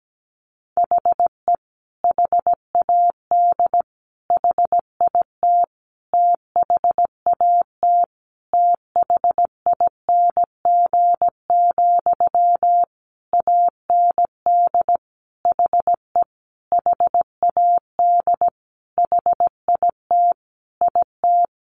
Morse code, heard as a keyed tone, about 17 wpm, 700 Hz. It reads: HE HAD HIT THAT THING, AND HE HAD HIT IT
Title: Sentences from Top 1200 Words 17wpm